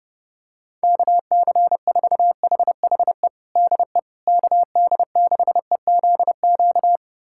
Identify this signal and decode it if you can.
Morse code, heard as a keyed tone, about 30 wpm, 700 Hz. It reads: KC4HHE DE KD6EZQ
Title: USA Callsign DE Callsign - 6 Characters - 30wpm